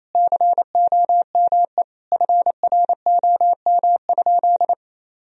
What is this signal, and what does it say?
Morse code, heard as a keyed tone, about 28 wpm, 700 Hz. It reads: COME FROM?
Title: Sentences from Top 1000 Words - Review Part 2 - 28wpm